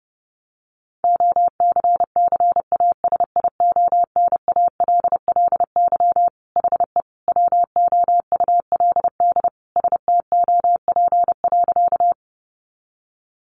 Morse code, heard as a keyed tone, about 28 wpm, 700 Hz. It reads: OCCASIONALLY HE WOULD STOP.
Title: Sentences from Top 1300 Words 30wpm